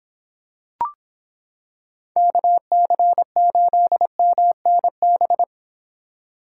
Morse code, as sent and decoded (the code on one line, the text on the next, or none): -.- -.-. ---.. -- -. -...
KC8MNB